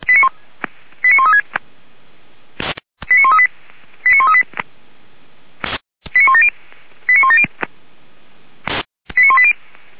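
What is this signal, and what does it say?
Five-tone selective call (ZVEI), heard as quick runs of five tones, tones of about 70 ms. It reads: 89136 89138 89138 89170 89170 89180
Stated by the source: ZVEI five-tone call 89110 received by SDR.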